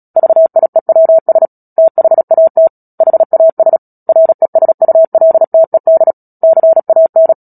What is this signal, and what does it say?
Morse code, heard as a keyed tone, about 37 wpm, 650 Hz. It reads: VIEWS THAT HAS RESULTED CAN
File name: Book Two / 10-Chapter 10 - Epilogue - 36 WPM